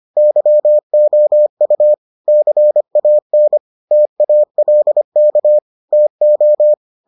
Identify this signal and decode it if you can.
Morse code, heard as a keyed tone, about 25 wpm, 600 Hz. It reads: YOU CAN TALK TO